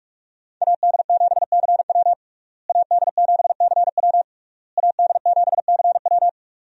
Morse code, heard as a keyed tone, about 43 wpm, 700 Hz. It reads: AD7CW AD7CW AD7CW
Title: USA Call Signs - 5 Characters - Repeated 3x - 45wpm